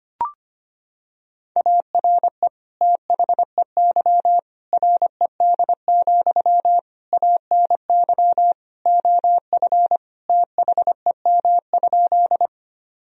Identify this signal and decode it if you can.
Morse code, heard as a keyed tone, about 25 wpm, 700 Hz. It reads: ARE THEY RED, ANY OF THEM?
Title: Sentences from Top 1000 Words - Review Part 2 - 25wpm